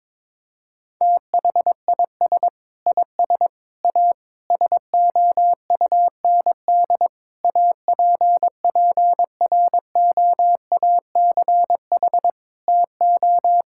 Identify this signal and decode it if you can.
Morse code, heard as a keyed tone, about 22 wpm, 700 Hz. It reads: THIS IS A SOUND APPROACH TO
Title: Sentences from Top 1000 Words - Review Part 6 - 22wpm